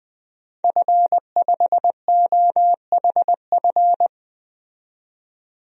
Morse code, heard as a keyed tone, about 20 wpm, 700 Hz. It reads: F5OHF
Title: DXCC-100 Call Signs - 5 Characters - 20wpm